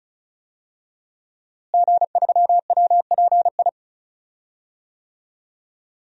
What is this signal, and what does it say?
Morse code, heard as a keyed tone, about 34 wpm, 700 Hz. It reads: G3WPI
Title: DXCC-100 Call Signs - 5 Characters - Repeated 2x - 35wpm